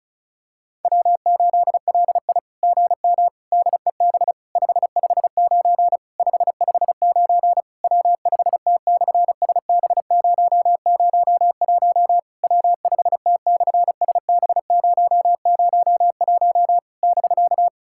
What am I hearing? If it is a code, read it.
Morse code, heard as a keyed tone, about 33 wpm, 700 Hz. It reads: W8LI GM DEB 559 559 W5T/SB001 W5T/SB001 <BK>
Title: SOTA QSO Practice 35wpm